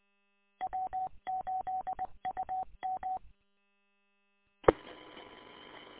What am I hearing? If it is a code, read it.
Morse code, heard as a keyed tone, about 21 wpm, 750 Hz. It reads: W8UM E
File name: w8um- -combined